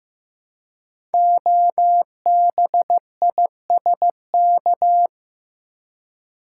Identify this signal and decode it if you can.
Morse code, heard as a keyed tone, about 15 wpm, 700 Hz. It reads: OBISK
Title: Code Groups - 5 Letters - 15wpm